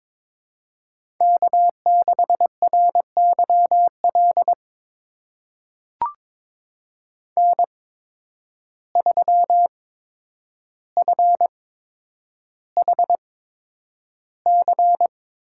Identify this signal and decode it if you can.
Morse code, heard as a keyed tone, about 22 wpm, spaced as for 9 wpm, 700 Hz. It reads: K6RYL N3FHC
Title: USA Call Signs - 5 Characters - 8x Character Spacing - 22wpm